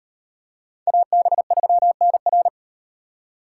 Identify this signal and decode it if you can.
Morse code, heard as a keyed tone, about 38 wpm, 700 Hz. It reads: AB3NR